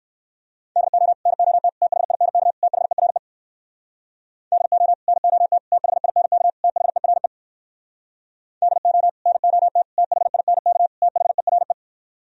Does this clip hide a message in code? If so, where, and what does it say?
Yes: Morse code from 0.8 to 11.7 s, heard as a keyed tone, about 58 wpm, 700 Hz. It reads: DO NOT THINK THERE DO NOT THINK THERE DO NOT THINK THERE